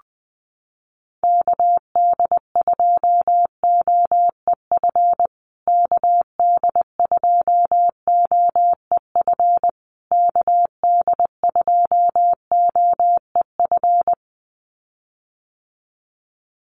Morse code, heard as a keyed tone, about 20 wpm, 700 Hz. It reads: KD2OEF KD2OEF KD2OEF
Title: USA Call Signs Repeated 3x 20wpm